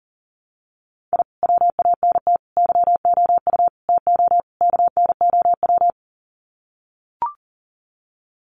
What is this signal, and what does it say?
Morse code, heard as a keyed tone, about 40 wpm, 700 Hz. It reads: I WANT YOU TO KNOW
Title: Sets of 5 Words Encore 40wpm